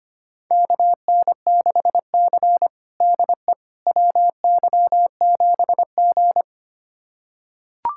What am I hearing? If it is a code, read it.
Morse code, heard as a keyed tone, about 25 wpm, 700 Hz. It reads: KN6C DE WY7G
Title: USA Callsign DE Callsign - 4 Characters - 25wpm